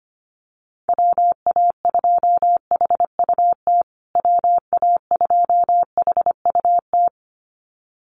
Morse code, heard as a keyed tone, about 25 wpm, 700 Hz. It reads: WA2HUT WA2HUT
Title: USA Call Signs Repeated 2x 25wpm